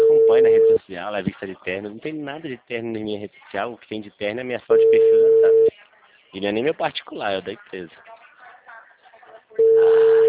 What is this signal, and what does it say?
Ringback tone (China, 450 Hz), with speech.
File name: out- -2013- - - .47919